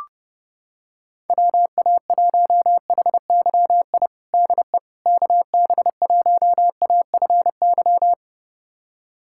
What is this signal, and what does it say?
Morse code, heard as a keyed tone, about 31 wpm, 700 Hz. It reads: WA1HYI DE KB1AFY
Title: USA Callsign DE Callsign - 6 Characters - 30wpm